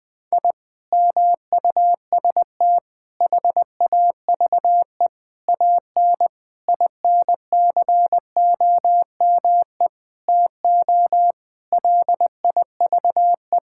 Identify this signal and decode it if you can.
Morse code, heard as a keyed tone, about 20 wpm, 700 Hz. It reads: I MUST HAVE AN INCOME TO LIVE
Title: Sentences from Top 1000 Words - Review Part 5 - 20wpm